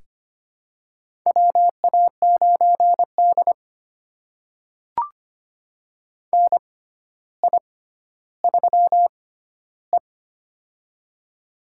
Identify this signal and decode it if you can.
Morse code, heard as a keyed tone, about 25 wpm, spaced as for 13 wpm, 700 Hz. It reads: WA9D NI3E
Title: USA Call Signs - 6x Character Spacing - 25wpm